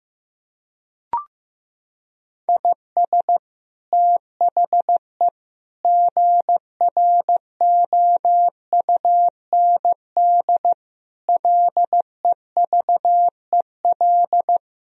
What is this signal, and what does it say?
Morse code, heard as a keyed tone, about 15 wpm, 700 Hz. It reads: IS THE GROUND LEVEL
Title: Sentences from Top 1000 Words - Review Part 4 - 15wpm